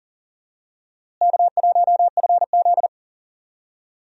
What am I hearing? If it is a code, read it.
Morse code, heard as a keyed tone, about 40 wpm, 700 Hz. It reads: K1FZ
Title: USA Call Signs - 4 Characters - Repeated 3x - 40wpm